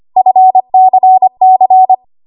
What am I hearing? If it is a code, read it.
Morse code, heard as a keyed tone, about 25 wpm, 750 Hz. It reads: FCC
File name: Archive / 25 WPM / 25WPM